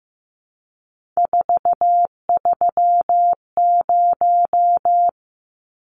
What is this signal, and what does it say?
Morse code, heard as a keyed tone, about 15 wpm, 700 Hz. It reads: 430
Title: Numbers - 3-Digit 15wpm